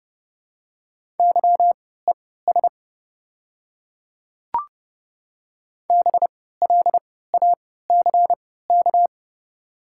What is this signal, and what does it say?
Morse code, heard as a keyed tone, about 31 wpm, spaced as for 17 wpm, 700 Hz. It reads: YES BLACK
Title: 3x Character Spacing - Top 500 Words - 30wpm